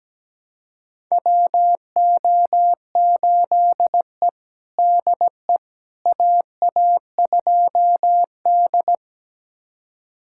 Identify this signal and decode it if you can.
Morse code, heard as a keyed tone, about 17 wpm, 700 Hz. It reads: WO8E DE AA2D
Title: USA Callsign DE Callsign - 4 Characters - 17wpm